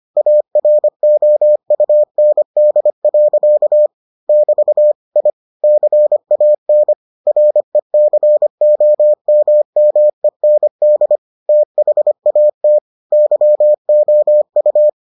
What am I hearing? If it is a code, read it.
Morse code, heard as a keyed tone, about 25 wpm, 600 Hz. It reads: AROUND. <BT> I CAN RECOMMEND THAT YOU